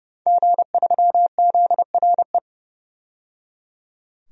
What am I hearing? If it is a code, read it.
Morse code, heard as a keyed tone, about 31 wpm, 700 Hz. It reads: G3ZRE
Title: DXCC-100 Call Signs - 5 Characters - 30wpm